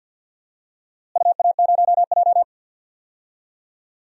Morse code, heard as a keyed tone, about 46 wpm, 700 Hz. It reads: UA0J
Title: DXCC-100 Call Signs - 4 Characters - Repeated 3x - 50wpm